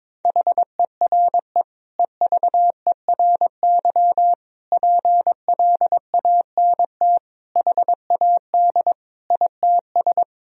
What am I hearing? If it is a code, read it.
Morse code, heard as a keyed tone, about 22 wpm, 700 Hz. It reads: HERE EVERY PLANT HAD ITS
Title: Sentences from Top 1000 Words - Review Part 1 - 22wpm